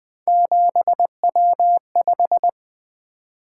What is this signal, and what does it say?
Morse code, heard as a keyed tone, about 23 wpm, 700 Hz. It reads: 7W5
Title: Speed-Racing - Three Letter-Number Combinations - 20wpm